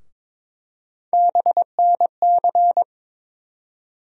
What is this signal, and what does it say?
Morse code, heard as a keyed tone, about 22 wpm, 700 Hz. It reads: BNC
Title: USA Call Sign Postfixes - 3 Letters 22wpm